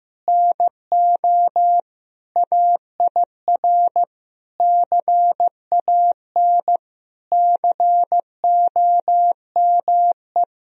Morse code, heard as a keyed tone, about 15 wpm, 700 Hz. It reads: NO AIR CAN COME
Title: Sentences from Top 1000 Words - Review Part 1 - 15wpm